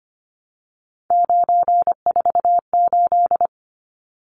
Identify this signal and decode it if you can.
Morse code, heard as a keyed tone, about 25 wpm, 700 Hz. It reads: 948
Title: Numbers - 3-Digit 25wpm